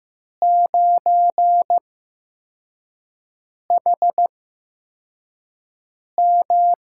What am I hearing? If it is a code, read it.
Morse code, heard as a keyed tone, about 15 wpm, spaced as for 4 wpm, 700 Hz. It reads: 9HM
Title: USA Call Signs - 4 Characters - 8x Character Spacing - 15wpm